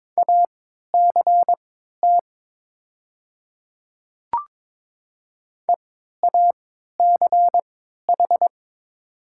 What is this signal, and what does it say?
Morse code, heard as a keyed tone, about 22 wpm, spaced as for 12 wpm, 700 Hz. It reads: ACT EACH
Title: 3x Character Spacing - Top 200 Words - 22wpm